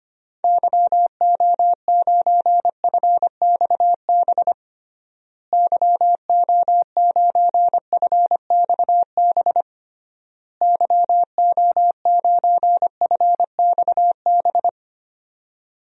Morse code, heard as a keyed tone, about 25 wpm, 700 Hz. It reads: YO9FXB YO9FXB YO9FXB
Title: DXCC-100 Call Signs - 6 Characters - Repeated 3x - 25wpm